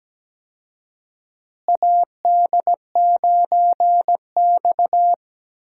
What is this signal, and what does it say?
Morse code, heard as a keyed tone, about 17 wpm, 700 Hz. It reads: AD9X
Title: USA Call Signs - 4 Characters - Repeated 2x - 17wpm